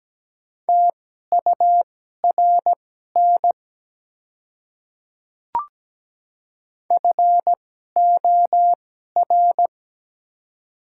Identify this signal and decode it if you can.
Morse code, heard as a keyed tone, about 17 wpm, spaced as for 12 wpm, 700 Hz. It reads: TURN FOR
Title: 2x Character Spacing - Top 200 Words - 17wpm